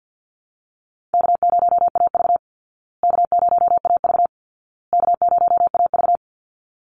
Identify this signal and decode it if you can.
Morse code, heard as a keyed tone, about 53 wpm, 700 Hz. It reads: K0AV K0AV K0AV
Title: Call Signs Repeated 2x - 50wpm